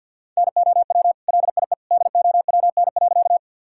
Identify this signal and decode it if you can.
Morse code, heard as a keyed tone, about 50 wpm, 700 Hz. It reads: NOW LIE DOWN.
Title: Sentences from Top 1600 Words 50wpm